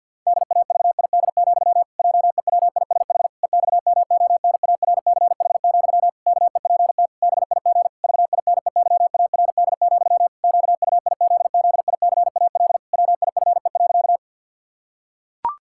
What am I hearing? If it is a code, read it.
Morse code, heard as a keyed tone, about 50 wpm, 700 Hz. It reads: DAVID, JEWISH EXMONARCH, KEPT BIG VINEYARD, QUIZZICAL WIFE.
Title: Pangram Challenge v2 - 50wpm